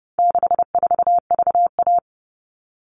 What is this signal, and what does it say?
Morse code, heard as a keyed tone, about 30 wpm, 700 Hz. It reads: 64VA
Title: All Cut Numbers - 4-Digit 30wpm